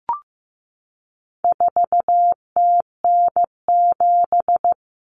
Morse code, heard as a keyed tone, about 15 wpm, 700 Hz. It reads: 4TN7
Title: Common Cut Numbers - 4-Digit 15wpm